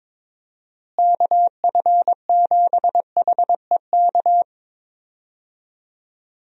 Morse code, heard as a keyed tone, about 22 wpm, 700 Hz. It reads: KF7HEK
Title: USA Call Signs - 6 Characters - Repeated 3x - 22wpm